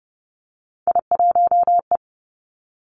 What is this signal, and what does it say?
Morse code, heard as a keyed tone, about 30 wpm, 700 Hz. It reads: I1E
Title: Three Challenging Letter-Number Combinations 30wpm